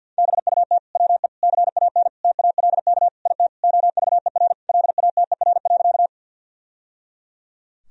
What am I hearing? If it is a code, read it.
Morse code, heard as a keyed tone, about 48 wpm, 700 Hz. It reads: BUT WE CAN TALK IT OVER LATER.
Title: Sentences from Top 1000 Words - Review Part 4 - 50wpm